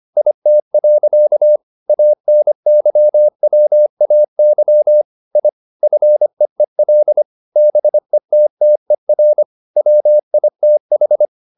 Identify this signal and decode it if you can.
Morse code, heard as a keyed tone, about 25 wpm, 600 Hz. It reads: IT. ANYWAY I FEEL BETTER WITH
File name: the old man and the sea part 1 25wpm